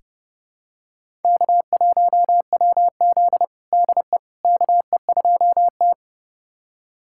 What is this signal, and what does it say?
Morse code, heard as a keyed tone, about 31 wpm, 700 Hz. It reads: K1WZ DE KE2T
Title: USA Callsign DE Callsign - 4 Characters - 30wpm